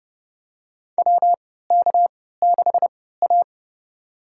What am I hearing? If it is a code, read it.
Morse code, heard as a keyed tone, about 29 wpm, spaced as for 17 wpm, 700 Hz. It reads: WK6A
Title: USA Call Signs - 4 Characters - 3x Character Spacing - 30wpm